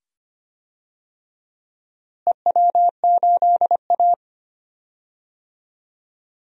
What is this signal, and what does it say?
Morse code, heard as a keyed tone, about 25 wpm, 700 Hz. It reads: EW8A
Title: DXCC-100 Call Signs - 4 Characters - Repeated 2x - 25wpm